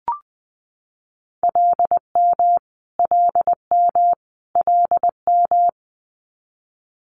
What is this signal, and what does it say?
Morse code, heard as a keyed tone, about 20 wpm, 700 Hz. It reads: LM LM LM
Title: Multi-letter Phonemes Repeated 3x 20wpm